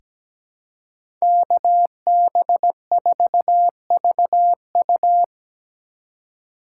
Morse code, heard as a keyed tone, about 17 wpm, 700 Hz. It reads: KB4VU
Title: K1USN SST QSO Practice - 17wpm